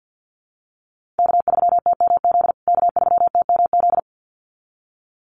Morse code, heard as a keyed tone, about 50 wpm, 700 Hz. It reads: K3TMZ K3TMZ
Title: USA Call Signs Repeated 2x 50wpm